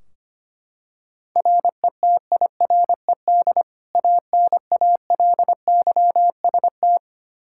Morse code, heard as a keyed tone, about 25 wpm, 700 Hz. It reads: RETIRED ANALYST
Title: Practice QSO - Exchange 3 - More Info - 25wpm